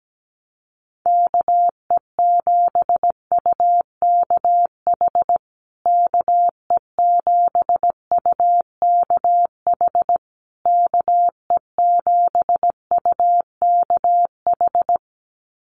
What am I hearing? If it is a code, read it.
Morse code, heard as a keyed tone, about 17 wpm, 700 Hz. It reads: KE7UKH KE7UKH KE7UKH
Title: USA Call Signs Repeated 3x 17wpm